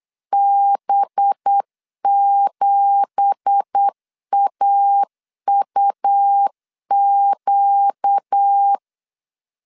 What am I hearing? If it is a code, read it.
Morse code, heard as a keyed tone, about 8 wpm, 800 Hz. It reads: B7AUQ
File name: CW / DET / 1 9 1500C S B 8wpm 5624